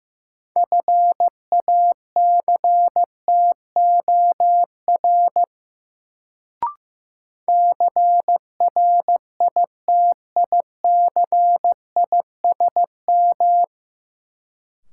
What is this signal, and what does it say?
Morse code, heard as a keyed tone, about 15 wpm, 700 Hz. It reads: FACTOR CRITICISM
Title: Sentences from Top 1300 Words - New Word Drill - 15wpm